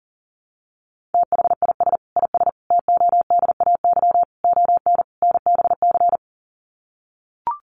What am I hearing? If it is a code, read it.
Morse code, heard as a keyed tone, about 40 wpm, 700 Hz. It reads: THIS IS TODAY ON NBC
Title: Sets of 5 Words Encore 40wpm